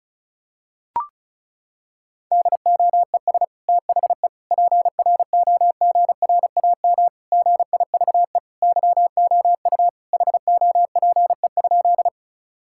Morse code, heard as a keyed tone, about 36 wpm, 700 Hz. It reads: DOES THE PROGRAM GIVE YOU HOPE?
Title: Sentences from Top 1000 Words - Review Part 5 - 35wpm